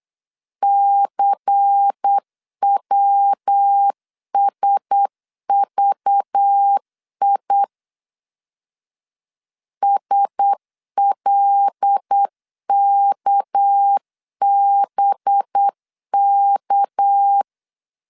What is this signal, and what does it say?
Morse code, heard as a keyed tone, about 8 wpm, 800 Hz. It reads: CWSVI SLKBK
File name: LEZIONI / 6 PKX0 / 6 PK 8wpm 3003